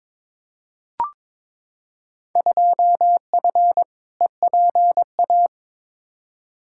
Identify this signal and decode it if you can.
Morse code, heard as a keyed tone, about 23 wpm, 700 Hz. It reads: 2F EPA
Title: Field Day Exchanges - 22wpm